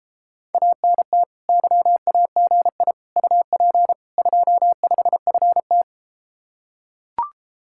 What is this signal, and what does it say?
Morse code, heard as a keyed tone, about 33 wpm, 700 Hz. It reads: ANT YAGI UP 25FT